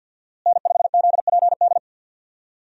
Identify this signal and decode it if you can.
Morse code, heard as a keyed tone, about 50 wpm, 700 Hz. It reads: N5ZPD
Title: USA Call Signs - 5 Characters - Repeated 2x - 50wpm